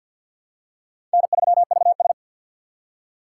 Morse code, heard as a keyed tone, about 53 wpm, 700 Hz. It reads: N3VS
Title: USA Call Signs - 4 Characters - 2x Character Spacing - 50wpm